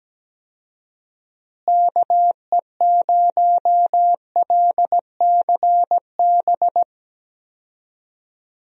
Morse code, heard as a keyed tone, about 17 wpm, 700 Hz. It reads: KE0LCB
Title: USA Call Signs - 6 Characters - Repeated 3x - 17wpm